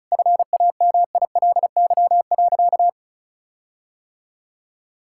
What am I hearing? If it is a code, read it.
Morse code, heard as a keyed tone, about 36 wpm, 700 Hz. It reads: FAMILY.
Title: Sentences from Top 1600 Words 35wpm